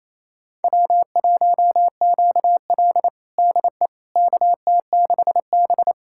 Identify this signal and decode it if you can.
Morse code, heard as a keyed tone, about 28 wpm, 700 Hz. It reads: W1QL DE KT6B
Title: USA Callsign DE Callsign - 4 Characters - 28wpm